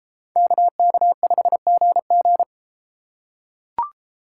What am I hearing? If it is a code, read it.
Morse code, heard as a keyed tone, about 32 wpm, 700 Hz. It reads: KK5GG